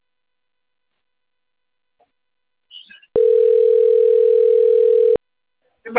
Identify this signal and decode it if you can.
Ringback tone (North America), 440 + 480 Hz.
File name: rg-1600-unknown- - - .38570